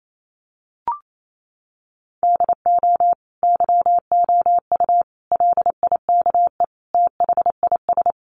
Morse code, heard as a keyed tone, about 28 wpm, 700 Hz. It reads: DO YOU LIKE THIS
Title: Sentences from Top 1200 Words 28wpm